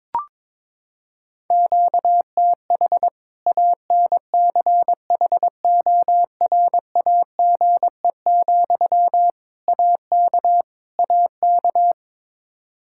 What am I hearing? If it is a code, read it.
Morse code, heard as a keyed tone, about 22 wpm, 700 Hz. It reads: QTH ANCHORAGE, AK AK
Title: K1USN SST QSO Practice 22wpm